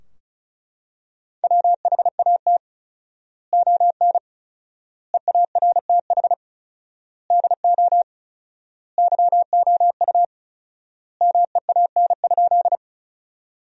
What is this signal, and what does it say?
Morse code, heard as a keyed tone, about 36 wpm, 700 Hz. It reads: WHAT ON EARTH DO YOU MEAN?
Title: 4x Word-Spacing - Sentences from Top 200 Words - 35wpm